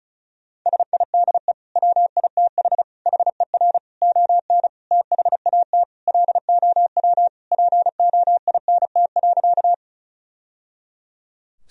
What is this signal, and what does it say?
Morse code, heard as a keyed tone, about 35 wpm, 700 Hz. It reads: SIDE WITH HER ON THAT LOW POINT.
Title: Sentences from Top 1000 Words - Review Part 1 - 35wpm